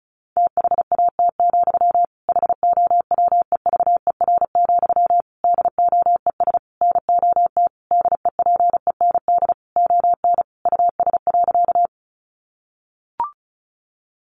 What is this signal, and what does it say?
Morse code, heard as a keyed tone, about 34 wpm, 700 Hz. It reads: THAT, HOWEVER, DOES NOT DEPEND ON US.
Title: Sentences from Top 1300 Words 35wpm